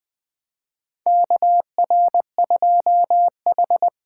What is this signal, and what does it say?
Morse code, heard as a keyed tone, about 21 wpm, 700 Hz. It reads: KR2H
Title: USA Call Signs - 4 Characters - 4x Character Spacing - 20wpm